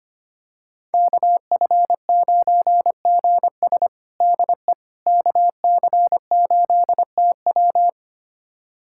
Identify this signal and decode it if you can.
Morse code, heard as a keyed tone, about 25 wpm, 700 Hz. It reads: KF9GS DE KC8TW
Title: USA Callsign DE Callsign - 5 Characters - 25wpm